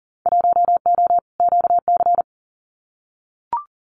Morse code, heard as a keyed tone, about 41 wpm, 700 Hz. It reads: 1O QC
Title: Winter Field Day Exchanges 40wpm